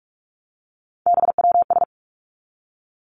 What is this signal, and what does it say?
Morse code, heard as a keyed tone, about 46 wpm, 700 Hz. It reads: BWS
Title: Three Challenging Letter-Number Combinations 45wpm